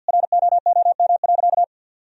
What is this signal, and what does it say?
Morse code, heard as a keyed tone, about 51 wpm, 700 Hz. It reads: ROOM.
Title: Sentences from Top 1000 Words - Review Part 1 - 50wpm